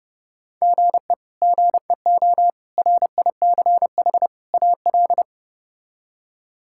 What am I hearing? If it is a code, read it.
Morse code, heard as a keyed tone, about 30 wpm, 700 Hz. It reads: GE GEO RICH AL
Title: K1USN SST QSO Practice - 30wpm